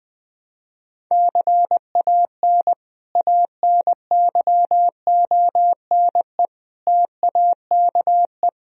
Morse code, heard as a keyed tone, about 20 wpm, 700 Hz. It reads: CAN ANYONE TAKE
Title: Sentences from Top 1000 Words - Review Part 4 - 20wpm